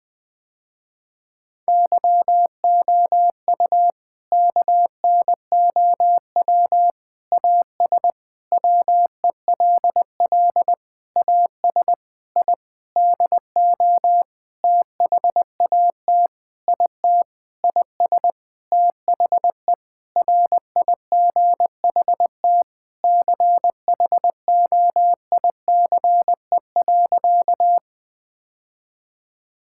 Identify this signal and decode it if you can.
Morse code, heard as a keyed tone, about 21 wpm, 700 Hz. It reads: YOU KNOW AS WELL AS I DO THAT IT IS THE RIGHT CHOICE.
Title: Sentences with 8-Word Phrases - Spoken Beginning - 20wpm